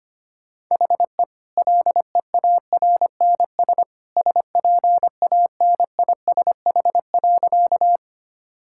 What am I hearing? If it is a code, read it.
Morse code, heard as a keyed tone, about 25 wpm, 690 Hz. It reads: HE LEARNS SPANISH.
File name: Easy Sentences with Hints - 25wpm